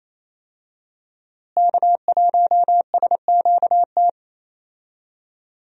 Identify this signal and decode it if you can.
Morse code, heard as a keyed tone, about 28 wpm, 700 Hz. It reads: K1SQT
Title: USA Call Signs - 5 Characters - Repeated 3x - 28wpm